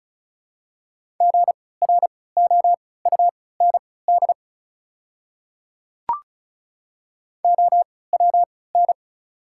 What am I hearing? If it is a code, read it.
Morse code, heard as a keyed tone, about 36 wpm, spaced as for 20 wpm, 700 Hz. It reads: GROUND OWN
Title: 3x Character Spacing - Top 500 Words - 35wpm